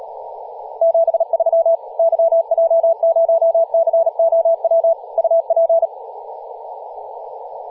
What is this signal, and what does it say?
Morse code, heard as a keyed tone, about 37 wpm, 660 Hz, with radio noise. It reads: 73 YJ0COW UP
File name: posts / 2016-aug-26-2236z-yj0cow-18-073mhz-cw